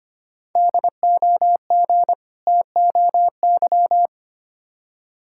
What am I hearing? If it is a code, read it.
Morse code, heard as a keyed tone, about 25 wpm, 700 Hz. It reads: DOG TOY
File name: practice / 25wpm / Difficult Words with Hints - 25wpm